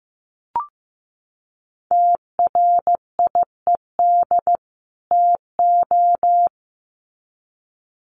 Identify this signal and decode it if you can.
Morse code, heard as a keyed tone, about 15 wpm, 700 Hz. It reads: TRIED TO
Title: Sets of 2 Words 15wpm